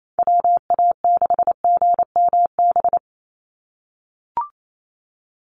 Morse code, heard as a keyed tone, about 28 wpm, 700 Hz. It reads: WA6GMB
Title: USA Call Signs 28wpm